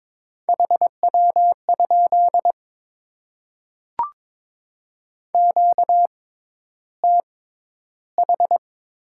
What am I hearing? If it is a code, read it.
Morse code, heard as a keyed tone, about 22 wpm, spaced as for 11 wpm, 700 Hz. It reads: HW? QTH
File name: Top 20 QSO Elements - 6x Character Spacing - 22wpm